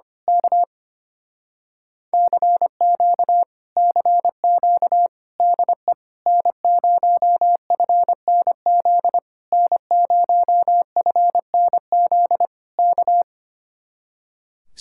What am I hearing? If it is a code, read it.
Morse code, heard as a keyed tone, about 24 wpm, 700 Hz. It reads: K CQ CQ DE N0FNZ N0FNZ K
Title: Speed-Racing Common QSO Phrases - 25wpm